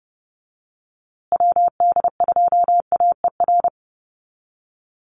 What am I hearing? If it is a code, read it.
Morse code, heard as a keyed tone, about 30 wpm, 700 Hz. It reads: WD2AER
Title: USA Call Signs Repeated 3x 30wpm